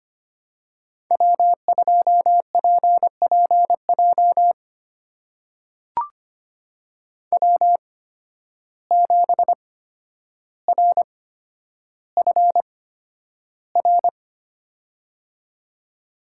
Morse code, heard as a keyed tone, about 25 wpm, spaced as for 11 wpm, 700 Hz. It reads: W2PPJ W7RFR
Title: USA Call Signs - 8x Character Spacing - 25wpm